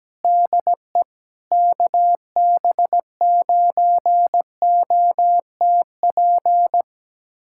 Morse code, heard as a keyed tone, about 17 wpm, 700 Hz. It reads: DE KB9OTP
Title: USA Callsign DE Callsign - 6 Characters - 17wpm